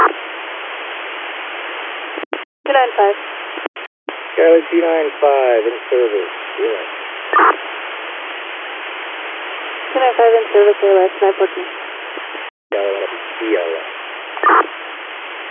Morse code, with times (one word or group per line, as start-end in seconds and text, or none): none